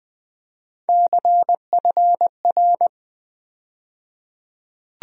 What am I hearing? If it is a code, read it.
Morse code, heard as a keyed tone, about 20 wpm, 700 Hz. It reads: CFR
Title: USA Call Sign Postfixes - 3 Letters 20wpm